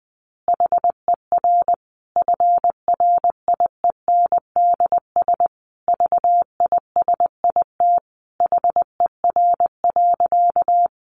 Morse code, heard as a keyed tone, about 20 wpm, 700 Hz. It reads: HER FRIENDS VISIT HER.
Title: Easy Phrases 20wpm